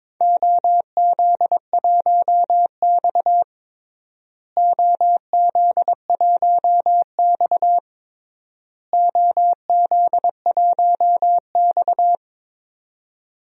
Morse code, heard as a keyed tone, about 22 wpm, 700 Hz. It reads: OZ1X OZ1X OZ1X
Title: DXCC-100 Call Signs - 4 Characters - Repeated 3x - 22wpm